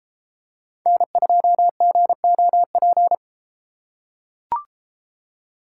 Morse code, heard as a keyed tone, about 33 wpm, 700 Hz. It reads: N2GOP